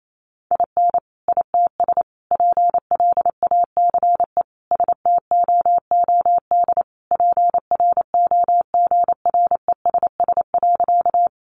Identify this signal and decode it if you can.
Morse code, heard as a keyed tone, about 28 wpm, 700 Hz. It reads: IN ITS PLACE STOOD PROGRESS.
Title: Sentences from Top 1400 Words 28wpm